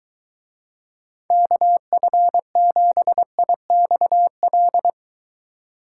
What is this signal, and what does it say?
Morse code, heard as a keyed tone, about 23 wpm, 700 Hz. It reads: KF7IXL